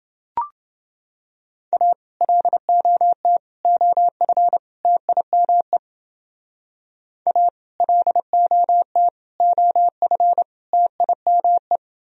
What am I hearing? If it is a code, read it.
Morse code, heard as a keyed tone, about 29 wpm, 700 Hz. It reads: A LOT OF TIME A LOT OF TIME